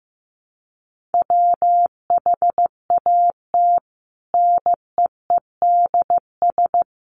Morse code, heard as a keyed tone, about 15 wpm, 700 Hz. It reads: WHAT NEEDS
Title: Sets of 5 Words Encore 15wpm